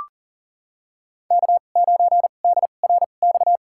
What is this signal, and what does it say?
Morse code, heard as a keyed tone, about 40 wpm, spaced as for 29 wpm, 700 Hz. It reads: K9DRX
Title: USA Call Signs - 2x Character Spacing - 40wpm